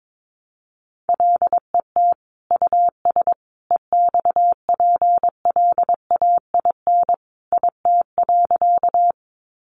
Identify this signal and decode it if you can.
Morse code, heard as a keyed tone, about 22 wpm, 700 Hz. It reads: LET US EXPLAIN IT.
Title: Sentences from Top 1100 Words 22wpm